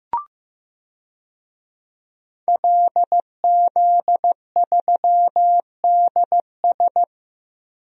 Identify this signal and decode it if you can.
Morse code, heard as a keyed tone, about 15 wpm, 700 Hz. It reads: LZ3DS
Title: DXCC-100 Call Signs - 5 Characters - Repeated 2x - 15wpm